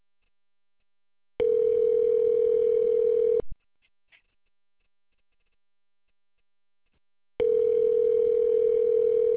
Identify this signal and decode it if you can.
Ringback tone (North America), 440 + 480 Hz.